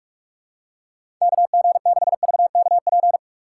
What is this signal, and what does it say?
Morse code, heard as a keyed tone, about 46 wpm, 700 Hz. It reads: KG6VKP
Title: USA Call Signs - 6 Characters - 2x Character Spacing - 45wpm